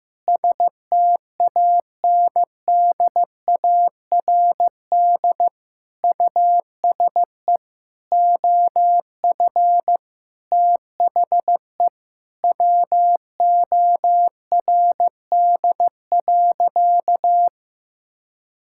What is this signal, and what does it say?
Morse code, heard as a keyed tone, about 15 wpm, 700 Hz. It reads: STANDARD USE OF THE WORD.
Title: Sentences from Top 1000 Words - Review Part 6 - 15wpm